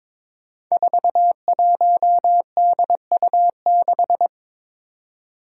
Morse code, heard as a keyed tone, about 22 wpm, 700 Hz. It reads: 41DU6
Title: Five Challenging Letter-Number Combinations 22wpm